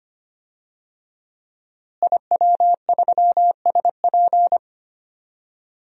Morse code, heard as a keyed tone, about 25 wpm, 700 Hz. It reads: IW3SP
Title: DXCC-100 Call Signs - 5 Characters - Repeated 2x - 25wpm